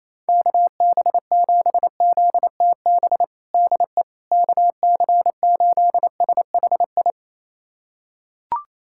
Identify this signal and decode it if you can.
Morse code, heard as a keyed tone, about 28 wpm, 700 Hz. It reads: KB7ZTB DE KC8SHI
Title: USA Callsign DE Callsign - 6 Characters - 28wpm